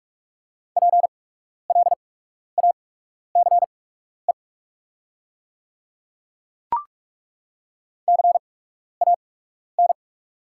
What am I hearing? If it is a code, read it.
Morse code, heard as a keyed tone, about 42 wpm, spaced as for 12 wpm, 690 Hz. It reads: PLACE CAN
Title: 8x Character Spacing - Top 200 Words - 45wpm